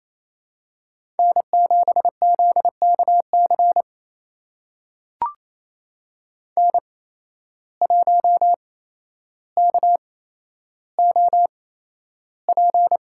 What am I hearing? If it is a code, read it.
Morse code, heard as a keyed tone, about 28 wpm, spaced as for 12 wpm, 700 Hz. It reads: N7ZKC N1KOP
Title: USA Call Signs - 5 Characters - 8x Character Spacing - 28wpm